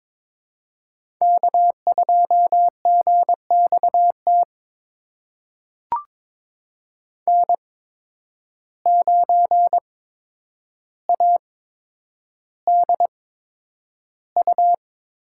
Morse code, heard as a keyed tone, about 22 wpm, spaced as for 9 wpm, 700 Hz. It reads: K2GXT N9ADU
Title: USA Call Signs - 5 Characters - 8x Character Spacing - 22wpm